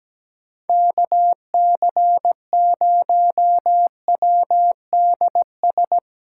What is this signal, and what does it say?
Morse code, heard as a keyed tone, about 17 wpm, 700 Hz. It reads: KC0WDS
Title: USA Callsign DE Callsign - 6 Characters - 17wpm